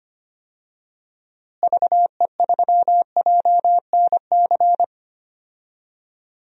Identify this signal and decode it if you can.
Morse code, heard as a keyed tone, about 25 wpm, 700 Hz. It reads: VE3JNC
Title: DXCC-100 Call Signs - 6 Characters - Repeated 2x - 25wpm